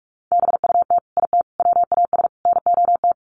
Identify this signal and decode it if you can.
Morse code, heard as a keyed tone, about 44 wpm, 700 Hz. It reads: BUT IT WAS NOT
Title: Sets of 4 Words 45wpm